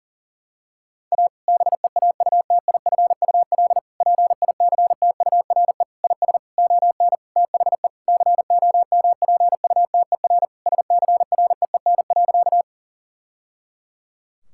Morse code, heard as a keyed tone, about 40 wpm, 700 Hz. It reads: A BEAUTIFUL PICTURE IS ON THE COMPUTER SCREEN.
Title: Sentences from Top 1000 Words - Review Part 6 - 40wpm